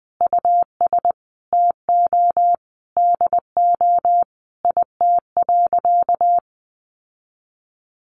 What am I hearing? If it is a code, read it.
Morse code, heard as a keyed tone, about 20 wpm, 700 Hz. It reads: US TO DO IT.
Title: Sentences from Top 1500 Words 20wpm